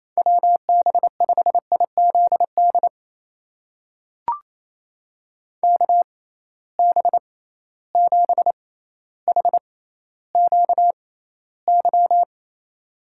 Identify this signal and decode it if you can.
Morse code, heard as a keyed tone, about 28 wpm, spaced as for 14 wpm, 700 Hz. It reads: WB5IZD KB7HQY
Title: USA Call Signs - 6 Characters - 6x Character Spacing - 28wpm